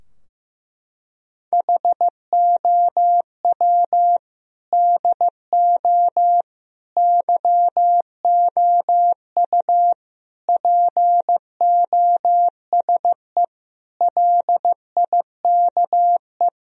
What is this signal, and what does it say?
Morse code, heard as a keyed tone, about 15 wpm, 700 Hz. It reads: HOW DO YOU POSE LIKE
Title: Sentences from Top 1000 Words - Review Part 2 - 15wpm